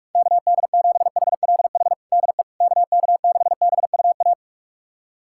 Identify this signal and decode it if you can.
Morse code, heard as a keyed tone, about 46 wpm, 700 Hz. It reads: KD7HLH DE KK6BUA
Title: USA Callsign DE Callsign - 6 Characters - 45wpm